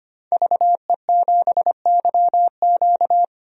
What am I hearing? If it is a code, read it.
Morse code, heard as a keyed tone, about 25 wpm, 700 Hz. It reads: VE7YQ
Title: DXCC-100 Call Signs - 5 Characters - 25wpm